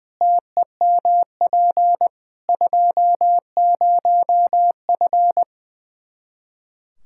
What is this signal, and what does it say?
Morse code, heard as a keyed tone, about 21 wpm, 700 Hz. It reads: TEMP 20F
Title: Practice QSO - Exchange 2 - Info - 20wpm